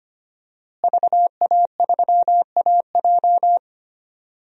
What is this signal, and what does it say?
Morse code, heard as a keyed tone, about 25 wpm, 700 Hz. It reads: VA3AJ
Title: DXCC-100 Call Signs - 5 Characters - Repeated 3x - 25wpm